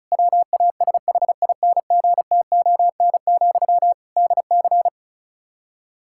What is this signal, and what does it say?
Morse code, heard as a keyed tone, about 34 wpm, 700 Hz. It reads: WASHINGTON, DC
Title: K1USN SST QSO Practice 35wpm